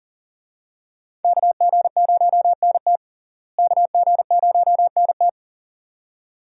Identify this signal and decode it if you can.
Morse code, heard as a keyed tone, about 38 wpm, 700 Hz. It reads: KG0NT KG0NT
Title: USA Call Signs - 5 Characters - Repeated 2x - 40wpm